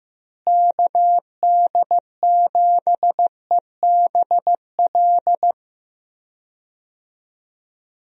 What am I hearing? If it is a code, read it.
Morse code, heard as a keyed tone, about 15 wpm, 700 Hz. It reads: KD7EBL
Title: USA Call Signs - 6 Characters - Repeated 3x - 15wpm